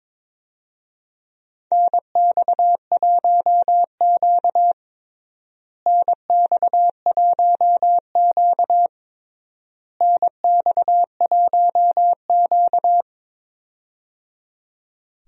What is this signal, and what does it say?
Morse code, heard as a keyed tone, about 22 wpm, 700 Hz. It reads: NX1Q NX1Q NX1Q
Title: USA Call Signs - 4 Characters - Repeated 3x - 22wpm